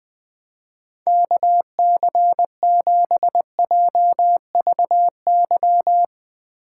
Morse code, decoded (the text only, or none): KC7JVY